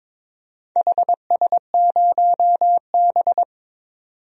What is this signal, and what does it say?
Morse code, heard as a keyed tone, about 22 wpm, 700 Hz. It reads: HS0B
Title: DXCC-100 Call Signs - 4 Characters - 22wpm